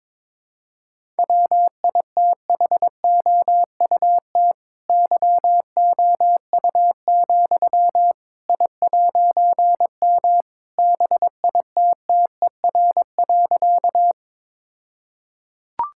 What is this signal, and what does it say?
Morse code, heard as a keyed tone, about 21 wpm, 700 Hz. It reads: WITHOUT YOU, I'M BITTER.
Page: morse-code-ninja-practice-22wpm